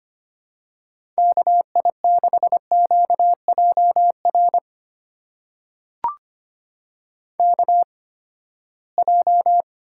Morse code, heard as a keyed tone, about 25 wpm, spaced as for 17 wpm, 700 Hz. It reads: KI6QJR KJ
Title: USA Call Signs - 6 Characters - 8x Character Spacing - 25wpm